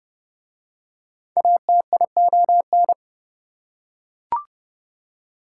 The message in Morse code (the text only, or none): ATION